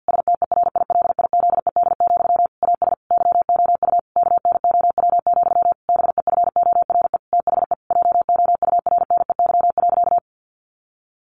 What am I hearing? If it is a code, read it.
Morse code, heard as a keyed tone, about 50 wpm, 700 Hz. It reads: STERILIZED, AS YOU KNOW, BEFORE THE JOURNEY.
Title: A Sound Of Thunder 50wpm